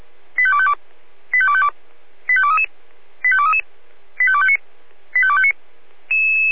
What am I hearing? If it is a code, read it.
Five-tone selective call (ZVEI), heard as quick runs of five tones, tones of about 70 ms. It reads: 85252 85252 85233 85233 85259 85259